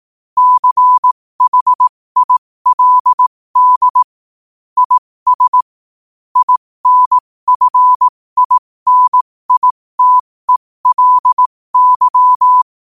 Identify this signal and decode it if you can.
Morse code, heard as a keyed tone, about 18 wpm, spaced as for 15 wpm, 1000 Hz. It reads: CHILD IS INFINITELY
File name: 15wpm